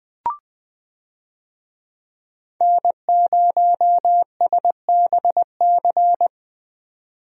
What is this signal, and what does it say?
Morse code, heard as a keyed tone, about 20 wpm, 700 Hz. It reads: N0SBC
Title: USA Call Signs - 5 Characters - 20wpm